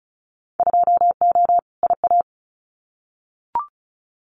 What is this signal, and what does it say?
Morse code, heard as a keyed tone, about 36 wpm, 700 Hz. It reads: 2O IA
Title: Winter Field Day Exchanges 35wpm